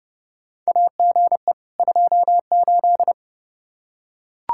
Morse code, heard as a keyed tone, about 32 wpm, 700 Hz. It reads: AGE 28
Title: K1USN SST QSO Practice 30wpm